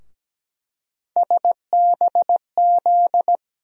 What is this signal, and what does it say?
Morse code, heard as a keyed tone, about 17 wpm, 700 Hz. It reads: SBZ
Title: USA Call Sign Postfixes - 3 Letters 17wpm